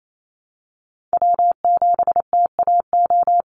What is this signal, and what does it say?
Morse code, heard as a keyed tone, about 28 wpm, 700 Hz. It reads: W7TAO
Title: NA SOTA Chaser Callsigns 28wpm